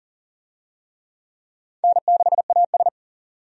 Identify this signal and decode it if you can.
Morse code, heard as a keyed tone, about 40 wpm, 700 Hz. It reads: N6AS
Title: USA Call Signs - 4 Characters - 40wpm